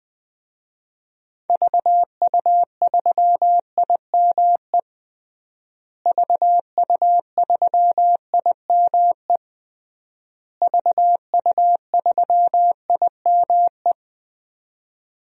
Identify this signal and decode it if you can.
Morse code, heard as a keyed tone, about 20 wpm, 700 Hz. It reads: VU3IME VU3IME VU3IME
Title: DXCC-100 Call Signs - 6 Characters - Repeated 3x - 20wpm